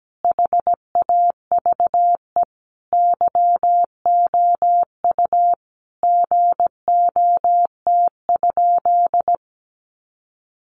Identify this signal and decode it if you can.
Morse code, heard as a keyed tone, about 17 wpm, 700 Hz. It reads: HAVE YOU GOT?
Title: Sentences from Top 300 Words 17wpm